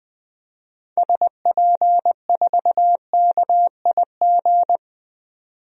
Morse code, heard as a keyed tone, about 20 wpm, 700 Hz. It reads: SP4KIG
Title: DXCC-100 Call Signs - 6 Characters - 20wpm